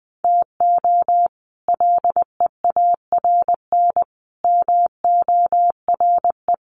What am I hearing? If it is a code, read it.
Morse code, heard as a keyed tone, about 20 wpm, 700 Hz. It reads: TO LEARN MORE
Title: Sets of 5 Words Encore2 20wpm